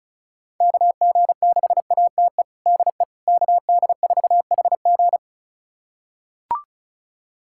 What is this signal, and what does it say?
Morse code, heard as a keyed tone, about 36 wpm, 700 Hz. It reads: KG6ATE DE KD4HG
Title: USA Callsign DE Callsign 35wpm